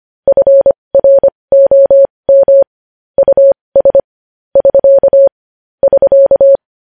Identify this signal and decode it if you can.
Morse code, heard as a keyed tone, about 25 wpm, spaced as for 20 wpm, 550 Hz. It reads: FROM US <SK> <SK>
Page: news-headlines-in-morse-code-at-20-wpm